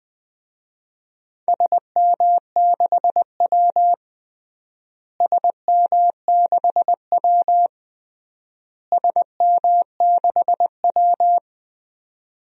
Morse code, heard as a keyed tone, about 21 wpm, 700 Hz. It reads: SM6W SM6W SM6W
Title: DXCC-100 Call Signs - 4 Characters - Repeated 3x - 20wpm